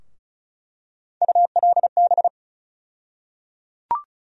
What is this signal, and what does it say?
Morse code, heard as a keyed tone, about 34 wpm, 700 Hz. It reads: ULB
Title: USA Call Sign Postfixes - 3 Letters 35wpm